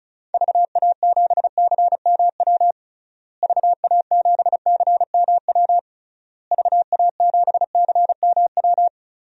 Morse code, heard as a keyed tone, about 36 wpm, 700 Hz. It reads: VA7CMW VA7CMW VA7CMW
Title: DXCC-100 Call Signs - 6 Characters - Repeated 3x - 35wpm